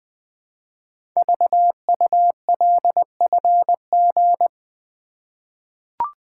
Morse code, heard as a keyed tone, about 20 wpm, 700 Hz. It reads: VULFG
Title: Code Groups - 5 Letters - 20wpm